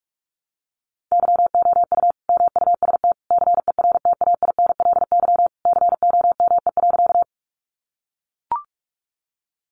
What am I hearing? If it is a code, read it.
Morse code, heard as a keyed tone, about 44 wpm, 700 Hz. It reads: YOU MUST CERTAINLY COME.
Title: Sentences from Top 900 Words 45wpm